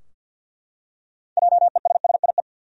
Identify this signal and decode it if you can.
Morse code, heard as a keyed tone, about 52 wpm, 700 Hz. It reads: JESSIE
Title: USA Common Names - 50wpm